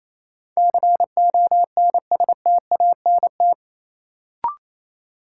Morse code, heard as a keyed tone, about 28 wpm, 700 Hz. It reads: CONSTANT
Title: Sentences from Top 1500 Words - New Word Drill Alt - 28wpm